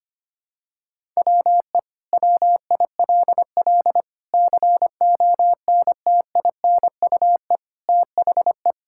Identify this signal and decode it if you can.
Morse code, heard as a keyed tone, about 25 wpm, 700 Hz. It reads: WE WILL CONTINUE THE
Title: Sentences from Top 1000 Words - Review Part 6 - 25wpm